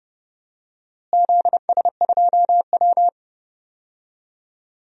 Morse code, heard as a keyed tone, about 30 wpm, 700 Hz. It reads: ZS2W
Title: DXCC-100 Call Signs - 4 Characters - Repeated 2x - 30wpm